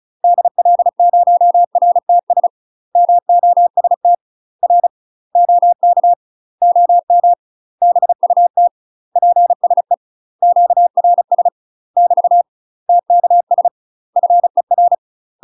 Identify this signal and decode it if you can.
Morse code, heard as a keyed tone, about 35 wpm, 700 Hz. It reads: DL0RTS MOST R OK OM BUT PSE QRS <BT> TKS FER
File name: UK MASTER QSOs TRAINING / 35WPM / UK MASTER QSOs 0001